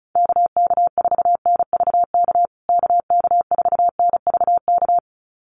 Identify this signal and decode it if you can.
Morse code, heard as a keyed tone, about 34 wpm, 700 Hz. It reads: KK4NVK KK4NVK
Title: USA Call Signs Repeated 2x 35wpm